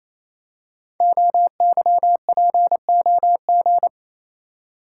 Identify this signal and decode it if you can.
Morse code, heard as a keyed tone, about 28 wpm, 690 Hz. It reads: OYPOG
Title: Code Groups - 5 Letters - 28wpm